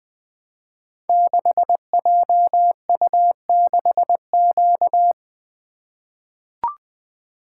Morse code, heard as a keyed tone, about 20 wpm, 700 Hz. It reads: 6JU6Q
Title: Code Groups - 5 Letter:Numbers - 20wpm